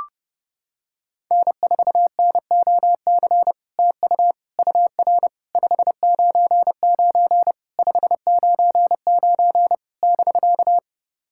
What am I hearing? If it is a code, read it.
Morse code, heard as a keyed tone, about 32 wpm, 700 Hz. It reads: N4NOC TU UR 599 599 <BK>
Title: SOTA QSO Practice 30wpm